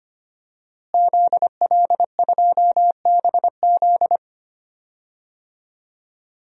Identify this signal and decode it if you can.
Morse code, heard as a keyed tone, about 25 wpm, 700 Hz. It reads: ZL2BZ
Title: DXCC-100 Call Signs - 5 Characters - Repeated 3x - 25wpm